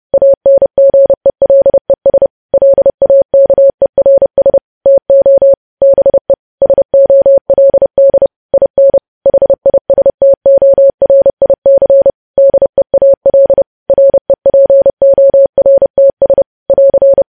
Morse code, heard as a keyed tone, about 30 wpm, 550 Hz. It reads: ANGELES LAKERS TO BE SOLD IN HISTORIC DEAL REPORTS <AR>
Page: news headlines in morse code at 30 wpm